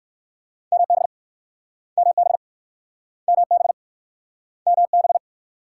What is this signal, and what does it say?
Morse code, heard as a keyed tone, about 58 wpm, 700 Hz. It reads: MB MB MB MB